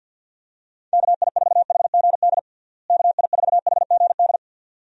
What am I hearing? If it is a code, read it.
Morse code, heard as a keyed tone, about 50 wpm, 700 Hz. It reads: KI4HGD KI4HGD
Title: USA Call Signs - 6 Characters - Repeated 2x - 50wpm